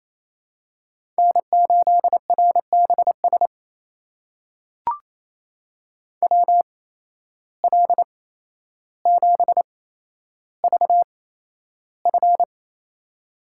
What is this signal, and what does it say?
Morse code, heard as a keyed tone, about 28 wpm, spaced as for 12 wpm, 700 Hz. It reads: N8RBS WL7VF
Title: USA Call Signs - 5 Characters - 8x Character Spacing - 28wpm